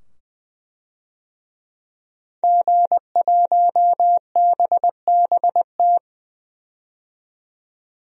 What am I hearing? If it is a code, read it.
Morse code, heard as a keyed tone, about 20 wpm, 700 Hz. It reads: G1BBT
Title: DXCC-100 Call Signs - 5 Characters - 20wpm